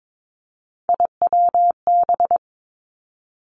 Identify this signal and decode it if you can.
Morse code, heard as a keyed tone, about 22 wpm, 700 Hz. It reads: IWB
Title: Three Challenging Letter-Number Combinations 22wpm